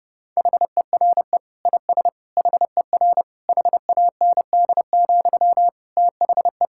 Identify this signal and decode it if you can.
Morse code, heard as a keyed tone, about 30 wpm, 700 Hz. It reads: HERE IS HER HAND, THE
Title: Sentences from Top 1600 Words 30wpm